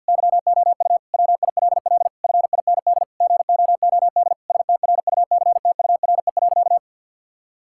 Morse code, heard as a keyed tone, about 53 wpm, 700 Hz. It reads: YOU WILL FIND GOOD STRUCTURE.
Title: Sentences from Top 1000 Words - Review Part 6 - 50wpm